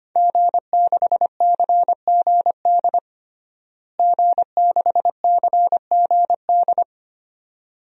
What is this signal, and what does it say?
Morse code, heard as a keyed tone, about 25 wpm, 700 Hz. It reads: G6CGD G6CGD
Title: DXCC-100 Call Signs - 5 Characters - Repeated 2x - 25wpm